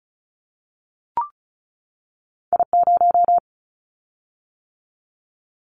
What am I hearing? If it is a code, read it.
Morse code, heard as a keyed tone, about 34 wpm, 700 Hz. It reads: I0
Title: Callsign Prefixes for Top 25 DXCC - 35wpm